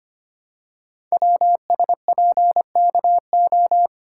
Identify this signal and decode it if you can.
Morse code, heard as a keyed tone, about 25 wpm, 700 Hz. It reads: WSPKO
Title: Code Groups - 5 Letters - 25wpm